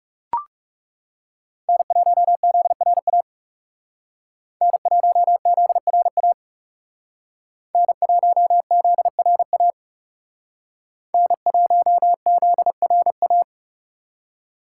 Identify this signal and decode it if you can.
Morse code, heard as a keyed tone, about 38 wpm, 700 Hz. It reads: N1ZRA N1ZRA N1ZRA N1ZRA